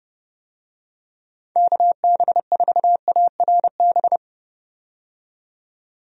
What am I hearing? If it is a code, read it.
Morse code, heard as a keyed tone, about 30 wpm, 700 Hz. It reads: KB4ARB
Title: USA Call Signs - 6 Characters - 30wpm